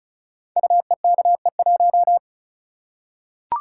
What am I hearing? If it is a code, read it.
Morse code, heard as a keyed tone, about 36 wpm, 700 Hz. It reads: UEKE1
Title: Code Groups - 5 Letter:Numbers - 35wpm